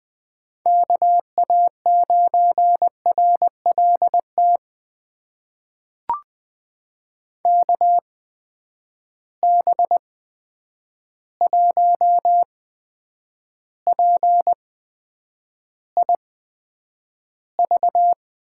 Morse code, heard as a keyed tone, about 20 wpm, spaced as for 9 wpm, 700 Hz. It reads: KA9RLT KB1PIV
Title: USA Call Signs - 6 Characters - 8x Character Spacing - 20wpm